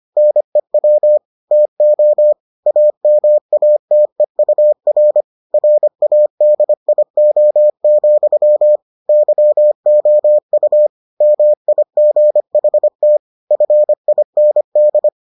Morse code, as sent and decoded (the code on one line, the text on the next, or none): -. .-- / - --- / .- -- .- - . ..- .-. / .-. .- -.. .. --- --..-- / -.-- --- ..- / -- .. --. .... - / ..-. .. -. -..
NEW TO AMATEUR RADIO, YOU MIGHT FIND